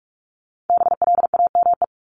Morse code, heard as a keyed tone, about 44 wpm, 700 Hz. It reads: BLAME
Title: Top 1751-2000 Words 45wpm